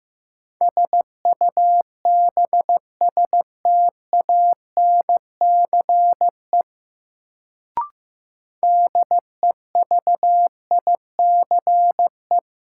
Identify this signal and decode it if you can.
Morse code, heard as a keyed tone, about 15 wpm, 700 Hz. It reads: SUBSTANCE DEVICE
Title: Sentences from Top 1500 Words - New Word Drill - 15wpm